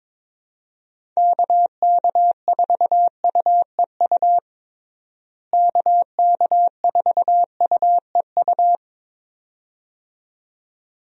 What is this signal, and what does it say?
Morse code, heard as a keyed tone, about 21 wpm, 700 Hz. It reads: KK4UEU KK4UEU
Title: USA Call Signs - 6 Characters - Repeated 2x - 22wpm